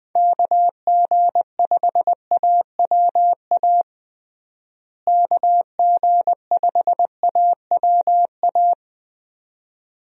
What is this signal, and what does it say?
Morse code, heard as a keyed tone, about 21 wpm, 700 Hz. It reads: KG5AWA KG5AWA
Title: USA Call Signs - 6 Characters - Repeated 2x - 20wpm